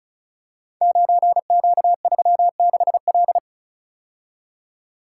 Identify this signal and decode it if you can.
Morse code, heard as a keyed tone, about 36 wpm, 690 Hz. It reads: 9Q36L
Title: Code Groups - 5 Letter:Numbers - 35wpm